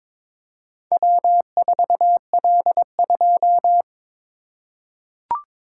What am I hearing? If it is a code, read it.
Morse code, heard as a keyed tone, about 22 wpm, 700 Hz. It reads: W4L2
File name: practice / 22wpm / Four Letter-Number Combinations - 22wpm